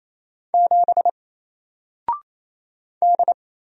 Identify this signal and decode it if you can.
Morse code, heard as a keyed tone, about 28 wpm, 700 Hz. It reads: 7 D
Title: Single Letter-Number 28wpm